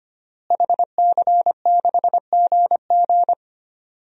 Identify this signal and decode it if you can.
Morse code, heard as a keyed tone, about 25 wpm, 700 Hz. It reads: HC6GG
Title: DXCC-100 Call Signs - 5 Characters - Repeated 2x - 25wpm